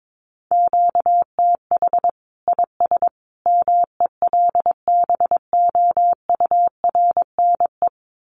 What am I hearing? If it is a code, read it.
Morse code, heard as a keyed tone, about 22 wpm, 700 Hz. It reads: QTH IS MELBOURNE
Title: Common QSO Phrases 22wpm